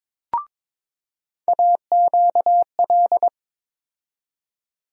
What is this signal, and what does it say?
Morse code, heard as a keyed tone, about 22 wpm, 700 Hz. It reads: AQL
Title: USA Call Sign Postfixes - 3 Letters 22wpm